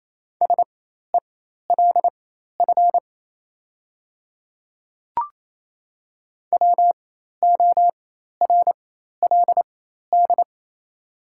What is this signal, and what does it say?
Morse code, heard as a keyed tone, about 27 wpm, spaced as for 13 wpm, 690 Hz. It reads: SELF WORLD
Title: 4x Character Spacing - Top 200 Words - 28wpm